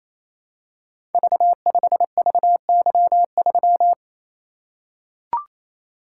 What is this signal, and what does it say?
Morse code, heard as a keyed tone, about 28 wpm, 700 Hz. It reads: V5VY3
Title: Code Groups - 5 Letter:Numbers - 28wpm